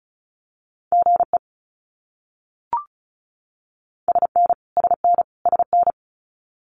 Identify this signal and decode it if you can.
Morse code, heard as a keyed tone, about 34 wpm, 700 Hz. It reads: GE SN SN SN
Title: Abbreviations Repeated 3x - 35wpm